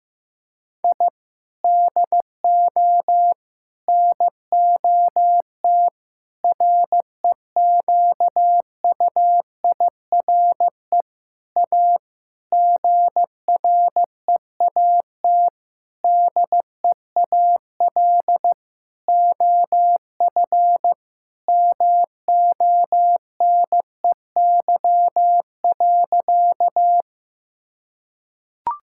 Morse code, heard as a keyed tone, about 15 wpm, 700 Hz. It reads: I DO NOT REQUIRE A GREAT DEAL OF MONEY.
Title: Sentences from Top 1000 Words - Review Part 6 - 15wpm